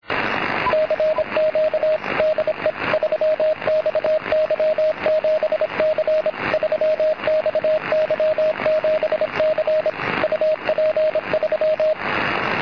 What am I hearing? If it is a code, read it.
Morse code, heard as a keyed tone, about 26 wpm, 620 Hz, with radio noise. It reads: CQ DE 3XY7C 3XY7C UP3
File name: 3XY7C CW